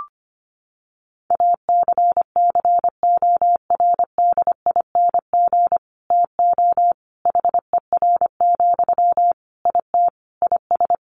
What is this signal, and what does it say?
Morse code, heard as a keyed tone, about 25 wpm, 700 Hz. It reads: ACCORDING TO HER, IT IS
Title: Sentences from Top 800 Words 25wpm